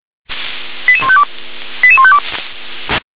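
Five-tone selective call (ZVEI), heard as quick runs of five tones, tones of about 70 ms. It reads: 88152 88152
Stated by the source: Five-tone ZVEI selective call received on an SDR.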